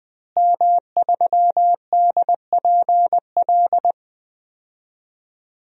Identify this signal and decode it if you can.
Morse code, heard as a keyed tone, about 20 wpm, 700 Hz. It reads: M3DPL
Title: DXCC-100 Call Signs - 5 Characters - Repeated 3x - 20wpm